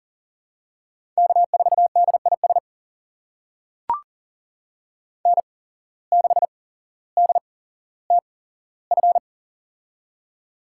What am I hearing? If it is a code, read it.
Morse code, heard as a keyed tone, about 39 wpm, spaced as for 17 wpm, 700 Hz. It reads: K4DIS N6DTF
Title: USA Call Signs - 5 Characters - 8x Character Spacing - 40wpm